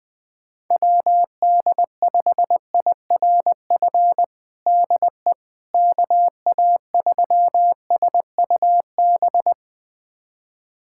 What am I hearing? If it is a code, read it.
Morse code, heard as a keyed tone, about 20 wpm, 700 Hz. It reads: WD5IRF DE KA3SUB
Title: USA Callsign DE Callsign - 6 Characters - 20wpm